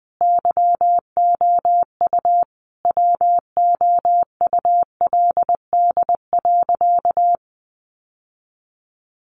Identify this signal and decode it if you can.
Morse code, heard as a keyed tone, about 20 wpm, 700 Hz. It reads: YOU WOULD.
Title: Sentences from Top 1500 Words 20wpm